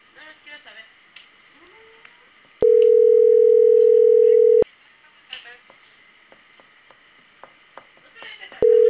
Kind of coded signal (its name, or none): telephone call progress tone